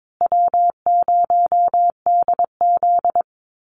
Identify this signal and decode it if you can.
Morse code, heard as a keyed tone, about 22 wpm, 700 Hz. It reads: W0DZ
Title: USA Call Signs Repeated 2x 22wpm